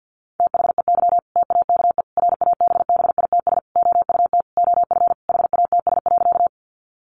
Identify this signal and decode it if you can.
Morse code, heard as a keyed tone, about 50 wpm, 700 Hz. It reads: THEY TAKE RABBITS OUT OF HATS.
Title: Easy Phrases 50wpm